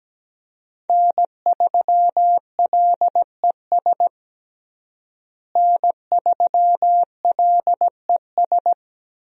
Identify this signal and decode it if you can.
Morse code, heard as a keyed tone, about 17 wpm, 700 Hz. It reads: N3LES N3LES
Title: USA Call Signs - 5 Characters - Repeated 2x - 17wpm